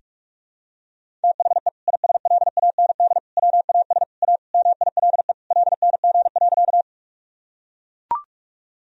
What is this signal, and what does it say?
Morse code, heard as a keyed tone, about 42 wpm, 690 Hz. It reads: THE ISLAND WAS A MILE LNG.
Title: Sentences from Top 500 Words and Abbreviations 45wpm